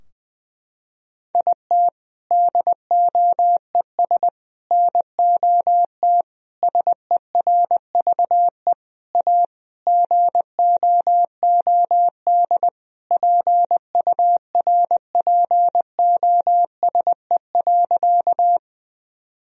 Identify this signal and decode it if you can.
Morse code, heard as a keyed tone, about 20 wpm, 700 Hz. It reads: IT DOES NOT SERVE A GOOD PURPOSE.
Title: Sentences from Top 1000 Words - Review Part 2 - 20wpm